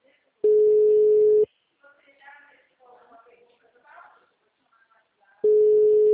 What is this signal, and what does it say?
Ringback tone (Europe, 425 Hz), with speech.